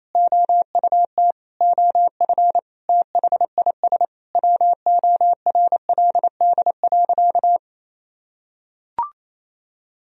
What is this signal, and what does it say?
Morse code, heard as a keyed tone, about 28 wpm, 700 Hz. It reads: OUT OF THIS WORLD.
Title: Sentences from Top 1000 Words - Review Part 1 - 28wpm